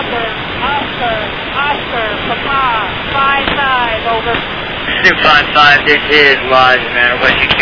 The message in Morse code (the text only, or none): none